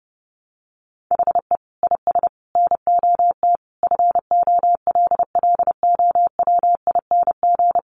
Morse code, heard as a keyed tone, about 30 wpm, 700 Hz. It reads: HE IS NOT FOLLOWING
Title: Sentences from Top 700 Words 30wpm